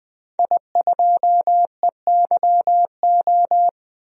Morse code, heard as a keyed tone, about 20 wpm, 700 Hz. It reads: I2EYO
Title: DXCC-100 Call Signs - 5 Characters - 20wpm